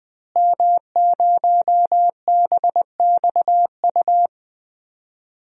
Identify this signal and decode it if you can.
Morse code, heard as a keyed tone, about 20 wpm, 700 Hz. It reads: M0BXU
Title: DXCC-100 Call Signs - 5 Characters - Repeated 3x - 20wpm